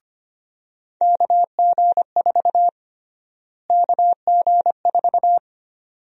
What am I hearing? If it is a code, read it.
Morse code, heard as a keyed tone, about 25 wpm, 700 Hz. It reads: KG4 KG4
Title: USA Call Sign Prefixes - Repeated 2x - 25wpm